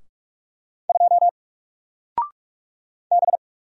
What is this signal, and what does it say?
Morse code, heard as a keyed tone, about 45 wpm, 700 Hz. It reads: 2 B
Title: Single Letter-Number Alt 45wpm